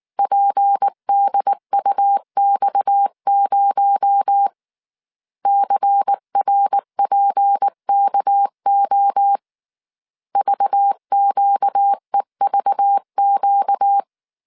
Morse code, heard as a keyed tone, about 19 wpm, 800 Hz. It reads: PDUX0 CRPKO VQEVQ
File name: CW / DET / 1 6 1500C S C 18wpm 2356